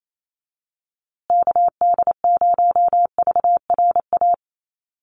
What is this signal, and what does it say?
Morse code, heard as a keyed tone, about 28 wpm, 700 Hz. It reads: KD0VRA
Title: USA Call Signs Repeated 2x 28wpm